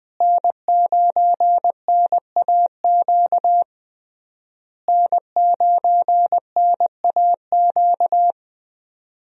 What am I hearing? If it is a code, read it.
Morse code, heard as a keyed tone, about 20 wpm, 700 Hz. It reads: N9NAQ N9NAQ
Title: USA Call Signs - 5 Characters - Repeated 3x - 20wpm